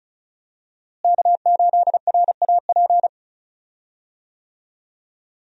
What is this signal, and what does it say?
Morse code, heard as a keyed tone, about 36 wpm, 700 Hz. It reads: K8RAP
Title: USA Call Signs - 5 Characters - Repeated 3x - 35wpm